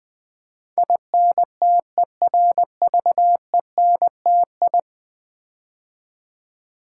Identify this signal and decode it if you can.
Morse code, heard as a keyed tone, about 20 wpm, 700 Hz. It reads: INTERVENTI
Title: Build A Word - 5 Letters At A Time - 12-Letter Words - 20wpm